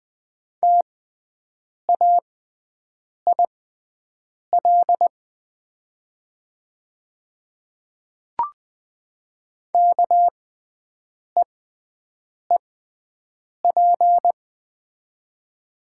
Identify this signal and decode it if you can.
Morse code, heard as a keyed tone, about 20 wpm, spaced as for 7 wpm, 700 Hz. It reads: TAIL KEEP
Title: 6x Character Spacing - Top 500 Words - 20wpm